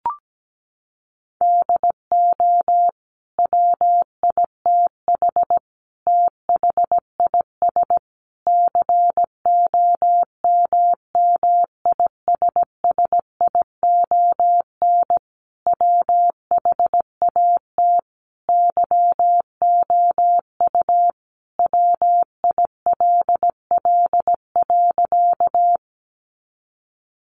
Morse code, heard as a keyed tone, about 17 wpm, 700 Hz. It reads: DO WITH THIS COMMISSION WHAT YOU WILL.
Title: Sentences from Top 1400 Words 17wpm